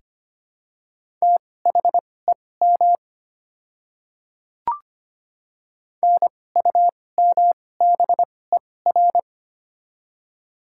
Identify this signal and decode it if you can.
Morse code, heard as a keyed tone, about 25 wpm, spaced as for 18 wpm, 700 Hz. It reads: THEM NUMBER
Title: 2x Character Spacing - Top 100 Words - 25wpm